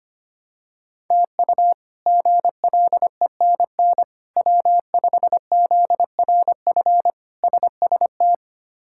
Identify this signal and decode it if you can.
Morse code, heard as a keyed tone, about 25 wpm, 700 Hz. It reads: TU GLENN W5ZRF SST
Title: K1USN SST QSO Practice - 25wpm